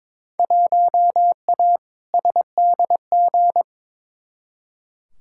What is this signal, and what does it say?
Morse code, heard as a keyed tone, about 22 wpm, 700 Hz. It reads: 1A SDG
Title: Field Day Exchanges Encore - 22wpm